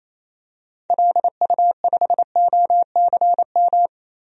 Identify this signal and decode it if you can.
Morse code, heard as a keyed tone, about 28 wpm, 700 Hz. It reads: LU5OCM
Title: DXCC-100 Call Signs - 6 Characters - Repeated 2x - 28wpm